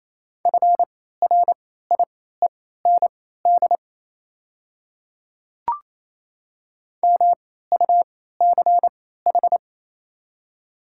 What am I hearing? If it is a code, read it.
Morse code, heard as a keyed tone, about 28 wpm, spaced as for 16 wpm, 700 Hz. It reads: FRIEND MUCH
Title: 3x Character Spacing - Top 500 Words - 28wpm